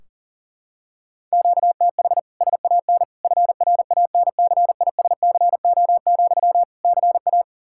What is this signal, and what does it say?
Morse code, heard as a keyed tone, about 40 wpm, 700 Hz. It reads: QTH SAN FRANCISCO, CA
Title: Practice QSO - Exchange 1 - Beginning - 40wpm